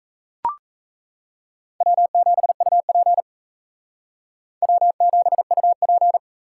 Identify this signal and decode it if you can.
Morse code, heard as a keyed tone, about 34 wpm, 700 Hz. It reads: W7UP W7UP
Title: Speed Racing - USA 1x2 Call Signs 28wpm